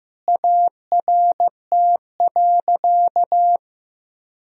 Morse code, heard as a keyed tone, about 15 wpm, 700 Hz. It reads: ART.
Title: Sentences from Top 1000 Words - Review Part 3 - 15wpm